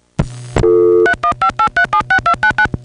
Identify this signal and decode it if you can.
DTMF, heard as telephone touch tones, tones of about 90 ms. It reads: 3158376299